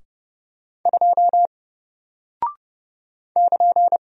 Morse code, heard as a keyed tone, about 30 wpm, 700 Hz. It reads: 2 <KN>
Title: Single Character (Letter:Number:Punctuation:Prosign) 30wpm